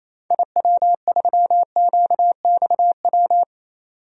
Morse code, heard as a keyed tone, about 28 wpm, 700 Hz. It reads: IW3QXW
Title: DXCC-100 Call Signs - 6 Characters - 28wpm